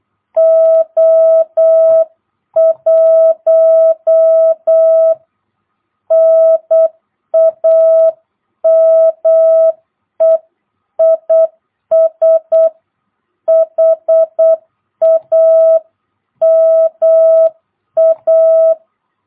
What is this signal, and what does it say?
Morse code, heard as a keyed tone, about 8 wpm, 660 Hz. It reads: O1 NAME IS HAMA
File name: jh8cbh / 2022cw